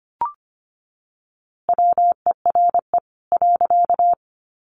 Morse code, heard as a keyed tone, about 25 wpm, 700 Hz. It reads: WERE .
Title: A Sound Of Thunder 25wpm